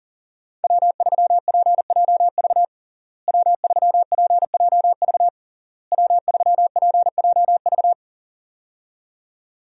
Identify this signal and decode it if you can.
Morse code, heard as a keyed tone, about 40 wpm, 700 Hz. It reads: W3PJV W3PJV W3PJV
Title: USA Call Signs - 5 Characters - Repeated 3x - 40wpm